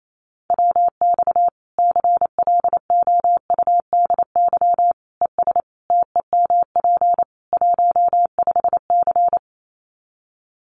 Morse code, heard as a keyed tone, about 28 wpm, 700 Hz. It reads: WX CLOUDY ES TEMP 15C
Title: Common QSO Phrases 28wpm